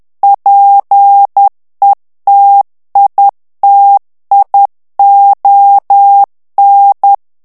Morse code, heard as a keyed tone, about 10 wpm, 800 Hz. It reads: PETITION
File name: ANGLAIS texte clair 3 vitesse 10 mots minute